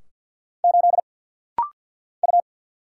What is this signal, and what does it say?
Morse code, heard as a keyed tone, about 49 wpm, 700 Hz. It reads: 8 U
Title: Single Letter-Number Alt - Rapid-Fire 50wpm